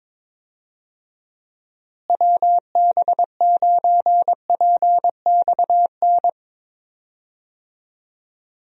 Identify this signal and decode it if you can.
Morse code, heard as a keyed tone, about 22 wpm, 700 Hz. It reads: WB9PXN
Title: USA Call Signs - 6 Characters - Repeated 3x - 22wpm